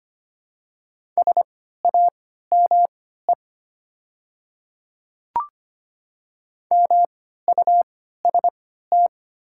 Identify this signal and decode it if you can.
Morse code, heard as a keyed tone, about 25 wpm, spaced as for 14 wpm, 700 Hz. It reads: SAME MUST
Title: 3x Character Spacing - Top 300 Words - 25wpm